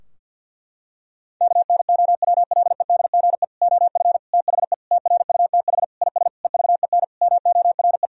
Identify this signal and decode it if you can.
Morse code, heard as a keyed tone, about 51 wpm, 700 Hz. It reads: KNOWLEDGE OF THE TRUTH IS EVEN MORE
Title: Sentences from Top 1000 Words - Review Part 3 - 50wpm